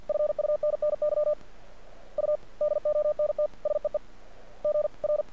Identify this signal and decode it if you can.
Morse code, heard as a keyed tone, about 48 wpm, 600 Hz. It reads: FUNNY U DONT SEE GR
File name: archived / CRSnet